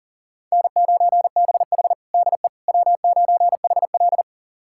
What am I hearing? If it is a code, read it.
Morse code, heard as a keyed tone, about 37 wpm, 700 Hz. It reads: N9BH DE W9HL
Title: USA Callsign DE Callsign - 4 Characters - 40wpm